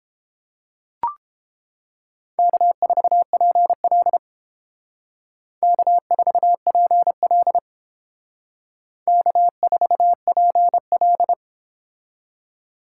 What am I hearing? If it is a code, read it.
Morse code, heard as a keyed tone, about 29 wpm, 700 Hz. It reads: K4PL K4PL K4PL